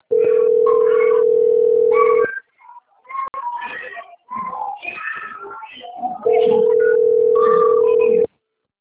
Ringback tone (North America), 440 + 480 Hz.
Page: Atibaia - SP 1 (CP Cartao)